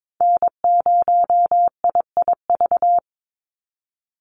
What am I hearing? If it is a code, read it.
Morse code, heard as a keyed tone, about 22 wpm, 700 Hz. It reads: N0IIV
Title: USA Call Signs 22wpm